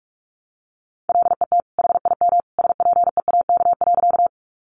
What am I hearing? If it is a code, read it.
Morse code, heard as a keyed tone, about 44 wpm, 700 Hz. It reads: LET HIM SPEAK.
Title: Sentences from Top 1100 Words 45wpm